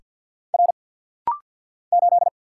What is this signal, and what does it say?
Morse code, heard as a keyed tone, about 49 wpm, 700 Hz. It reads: R 8
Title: Single Letter-Number - Rapid-Fire 50wpm